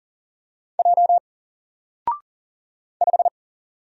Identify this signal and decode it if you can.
Morse code, heard as a keyed tone, about 40 wpm, 700 Hz. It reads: J 5
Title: Single Character (Letter:Number:Punctuation:Prosign) Alt 40wpm